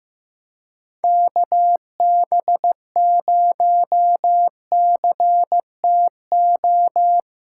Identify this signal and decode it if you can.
Morse code, heard as a keyed tone, about 15 wpm, 700 Hz. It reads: KB0CTO
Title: USA Call Signs - 6 Characters - 4x Character Spacing - 15wpm